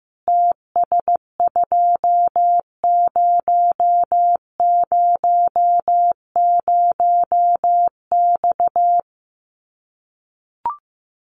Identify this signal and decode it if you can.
Morse code, heard as a keyed tone, about 15 wpm, 700 Hz. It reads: TS2000X
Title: Common QSO Phrases 15wpm